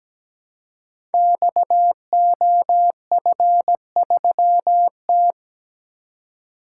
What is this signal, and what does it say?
Morse code, heard as a keyed tone, about 17 wpm, 700 Hz. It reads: XOF3T
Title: Code Groups - 5 Letter:Numbers - 17wpm